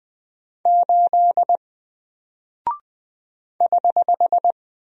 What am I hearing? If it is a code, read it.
Morse code, heard as a keyed tone, about 20 wpm, 700 Hz. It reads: 8 <HH>
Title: Single Character (Letter:Number:Punctuation:Prosign) 20wpm